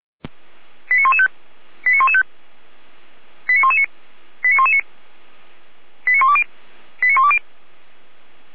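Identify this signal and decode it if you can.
Five-tone selective call (ZVEI), heard as quick runs of five tones, tones of about 70 ms. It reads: 89115 89115 89119 89119 89120 89120
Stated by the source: ZVEI five-tone selective call received on SDR.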